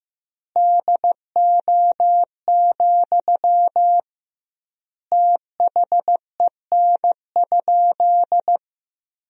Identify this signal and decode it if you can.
Morse code, heard as a keyed tone, about 15 wpm, 700 Hz. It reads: DO, THEN?
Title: 2x Word-Spacing - Sentences from Top 100 Words - 15wpm